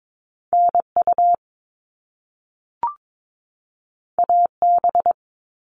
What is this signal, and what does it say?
Morse code, heard as a keyed tone, about 22 wpm, 700 Hz. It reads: NU AB
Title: Top Two Letter Combinations in Words 22wpm